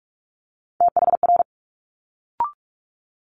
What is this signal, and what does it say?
Morse code, heard as a keyed tone, about 46 wpm, 700 Hz. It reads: THR
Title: Multi-letter Phonemes Quick Fire 45wpm